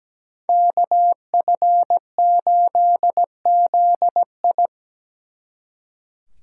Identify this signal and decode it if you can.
Morse code, heard as a keyed tone, about 20 wpm, 700 Hz. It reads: KF8ZI
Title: Speed Racing - USA Call Signs 17wpm